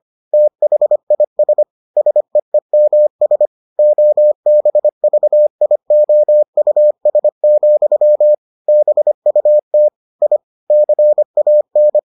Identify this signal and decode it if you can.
Morse code, heard as a keyed tone, about 25 wpm, 600 Hz. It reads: THIS SEEMS OBVIOUS, BUT I CAN